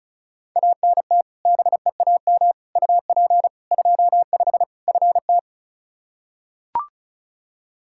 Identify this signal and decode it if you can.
Morse code, heard as a keyed tone, about 34 wpm, 700 Hz. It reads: ANT BEAM UP 25 FT
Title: K1USN SST QSO Practice 35wpm